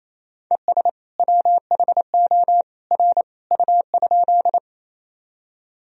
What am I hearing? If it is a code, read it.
Morse code, heard as a keyed tone, about 27 wpm, 700 Hz. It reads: ES WHO R U?
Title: Sentences from Top 100 Words and Abbreviations 28wpm